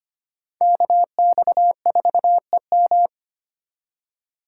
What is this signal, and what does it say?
Morse code, heard as a keyed tone, about 33 wpm, 700 Hz. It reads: KX4EM